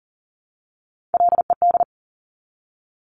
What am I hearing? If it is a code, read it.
Morse code, heard as a keyed tone, about 40 wpm, 690 Hz. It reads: LED
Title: Top Three Letter Combinations in Words 40wpm